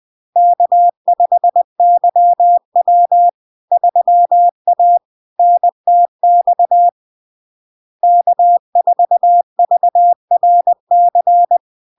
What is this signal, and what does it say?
Morse code, heard as a keyed tone, about 20 wpm, 700 Hz. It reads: K5YW 3A NTX K4VRC
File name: LARK-FD-2022-Log-in-CW-20wpm-700-Hz